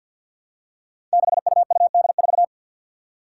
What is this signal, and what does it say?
Morse code, heard as a keyed tone, about 53 wpm, 700 Hz. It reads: 6UUD4
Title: Five Challenging Letter-Number Combinations 50wpm